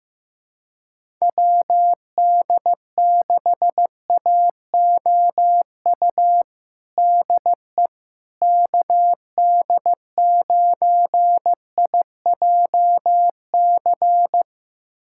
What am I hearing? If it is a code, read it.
Morse code, heard as a keyed tone, about 15 wpm, 700 Hz. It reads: WD6AOU DE KD9IJC
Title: USA Callsign DE Callsign - 6 Characters - 15wpm